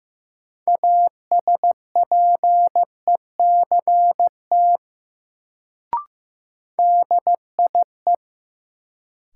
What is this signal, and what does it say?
Morse code, heard as a keyed tone, about 16 wpm, 700 Hz. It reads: ASPECT DIE
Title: Sentences from Top 1300 Words - New Word Drill Alt - 15wpm